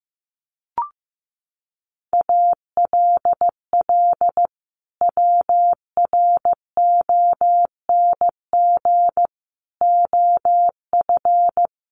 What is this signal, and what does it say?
Morse code, heard as a keyed tone, about 14 wpm, 700 Hz. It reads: ALL WRONG OF
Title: Sentences from Top 900 Words 15wpm